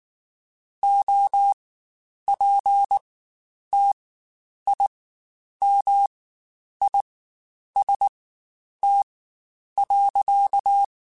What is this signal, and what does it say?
Morse code, heard as a keyed tone, about 19 wpm, spaced as for 9 wpm, 800 Hz. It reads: OPTIMIST.
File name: Quotes Spaced / Quotes Spaced-mp3s / Optimist sp @18